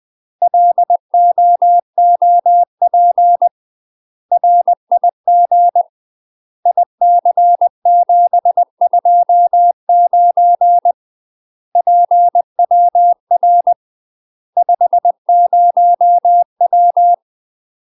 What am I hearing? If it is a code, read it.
Morse code, heard as a keyed tone, about 20 wpm, 700 Hz. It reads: LOOP RIG IC729 PWR 50W
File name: UK MASTER QSOs TRAINING / 20WPM / UK MASTER QSOs 0045